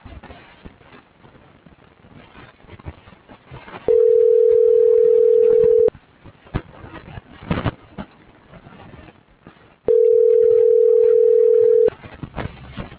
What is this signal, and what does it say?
Ringback tone (North America), 440 + 480 Hz.